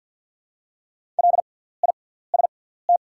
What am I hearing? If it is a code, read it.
Morse code, heard as a keyed tone, about 48 wpm, spaced as for 17 wpm, 700 Hz. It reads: LIST
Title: 6x Character Spacing - Top 500 Words - 50wpm